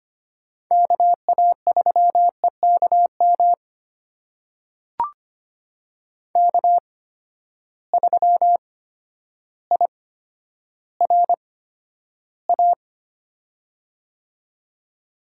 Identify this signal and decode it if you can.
Morse code, heard as a keyed tone, about 25 wpm, spaced as for 11 wpm, 700 Hz. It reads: KA3EKM K3IRA
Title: USA Call Signs - 8x Character Spacing - 25wpm